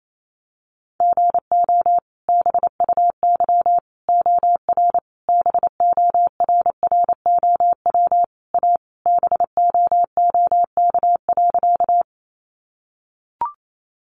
Morse code, heard as a keyed tone, about 28 wpm, 700 Hz. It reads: GO BUY OR BORROW A BOOK.
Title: Sentences from Top 1500 Words 28wpm